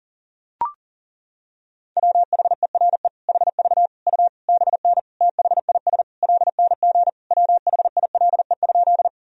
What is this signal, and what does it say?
Morse code, heard as a keyed tone, about 40 wpm, 700 Hz. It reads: WHERE HV U BN THIS LNG WHILE?
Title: Sentences from Top 300 Words and Abbreviations 40wpm